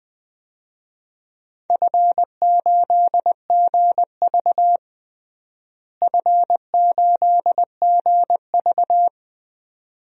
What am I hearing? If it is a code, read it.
Morse code, heard as a keyed tone, about 20 wpm, 700 Hz. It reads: F8GV F8GV
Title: DXCC-100 Call Signs - 4 Characters - Repeated 2x - 20wpm